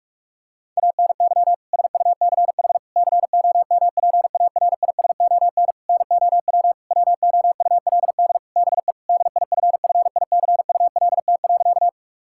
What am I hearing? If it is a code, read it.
Morse code, heard as a keyed tone, about 46 wpm, 700 Hz. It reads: ANY SUCH COMPARISON NOW WOULD BE DIFFICULT.
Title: Sentences from Top 1600 Words 45wpm